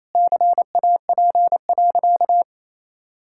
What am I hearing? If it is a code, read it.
Morse code, heard as a keyed tone, about 27 wpm, 700 Hz. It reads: CAP.
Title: Sentences from Top 1600 Words 28wpm